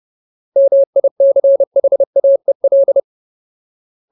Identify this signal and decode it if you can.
Morse code, heard as a keyed tone, about 30 wpm, 550 Hz. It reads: MICHAEL
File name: Ultimate Word List / 30WPM Ultimate